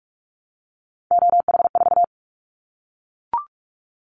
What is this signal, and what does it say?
Morse code, heard as a keyed tone, about 44 wpm, 700 Hz. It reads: OH4
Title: Callsign Prefixes for Top 25 DXCC - Repeated 3x - 45wpm